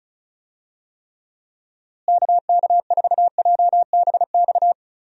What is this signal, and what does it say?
Morse code, heard as a keyed tone, about 36 wpm, 700 Hz. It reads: KK4JBX
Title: USA Call Signs - 6 Characters - Repeated 2x - 35wpm